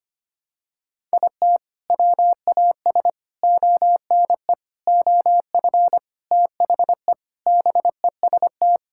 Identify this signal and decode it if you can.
Morse code, heard as a keyed tone, about 25 wpm, 700 Hz. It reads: IT WAS ONE OF THE BEST
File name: practice / 25wpm / Sentences with 3-Word Phrases - 25wpm